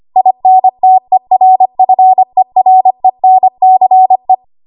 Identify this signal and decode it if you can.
Morse code, heard as a keyed tone, about 25 wpm, 750 Hz. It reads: INTERFERENCE
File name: Morse / 25WPM